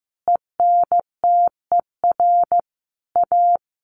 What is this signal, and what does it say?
Morse code, heard as a keyed tone, about 15 wpm, 700 Hz. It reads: ENTER A
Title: Sentences from Top 1300 Words 15wpm